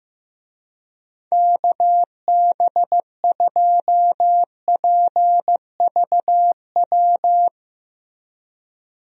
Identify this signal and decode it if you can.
Morse code, heard as a keyed tone, about 15 wpm, 700 Hz. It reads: KB2PVW
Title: USA Call Signs - 6 Characters - Repeated 2x - 15wpm